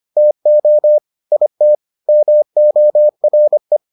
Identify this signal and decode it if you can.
Morse code, heard as a keyed tone, about 25 wpm, 600 Hz. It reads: TO IT MORE